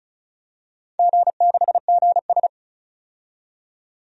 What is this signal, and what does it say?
Morse code, heard as a keyed tone, about 34 wpm, 700 Hz. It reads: G6GS
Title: DXCC-100 Call Signs - 4 Characters - 35wpm